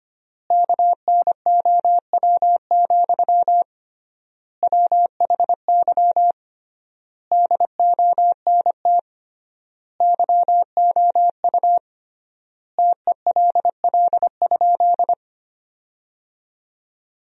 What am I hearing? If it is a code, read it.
Morse code, heard as a keyed tone, about 25 wpm, 700 Hz. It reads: KNOW, WHY DONT YOU TELL?
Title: 3x Word-Spacing - Sentences from Top 300 Words - 25wpm